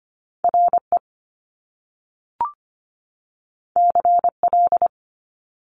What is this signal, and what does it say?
Morse code, heard as a keyed tone, about 25 wpm, 700 Hz. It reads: RE CL
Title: Top Two Letter Combinations in Words 25wpm